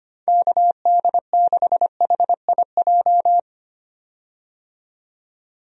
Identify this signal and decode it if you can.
Morse code, heard as a keyed tone, about 25 wpm, 700 Hz. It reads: KD6HIJ
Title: USA Call Signs - 6 Characters - 25wpm